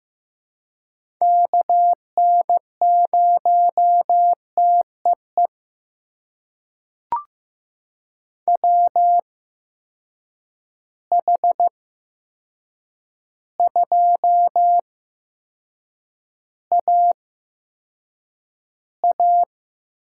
Morse code, heard as a keyed tone, about 15 wpm, spaced as for 7 wpm, 700 Hz. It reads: KN0TEE WH2AA
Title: USA Call Signs - 8x Character Spacing - 15wpm